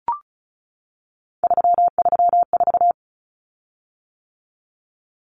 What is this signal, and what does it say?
Morse code, heard as a keyed tone, about 34 wpm, 700 Hz. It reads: 334
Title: Common Cut Numbers - 3-Digit 35wpm